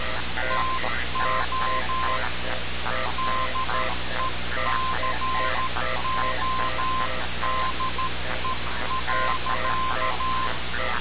Morse code, heard as a keyed tone, about 13 wpm, 1040 Hz, with noise. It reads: TO DECODE.
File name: radiocap / REC z